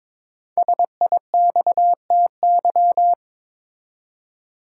Morse code, heard as a keyed tone, about 22 wpm, 700 Hz. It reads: SIXTY
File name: 5 Letter Words - First 3 Letters Given - 22wpm